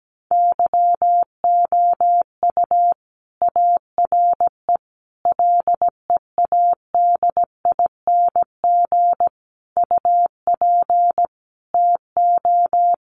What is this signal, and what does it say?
Morse code, heard as a keyed tone, about 17 wpm, 700 Hz. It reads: YOU ARE LEADING UP TO
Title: Sentences from Top 1100 Words 17wpm